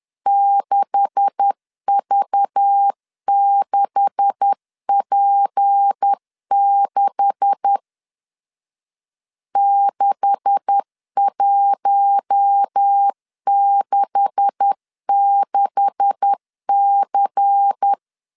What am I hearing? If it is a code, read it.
Morse code, heard as a keyed tone, about 10 wpm, 800 Hz. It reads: 6V6P6 6166C
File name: LEZIONI / 9 286 10wpm 2735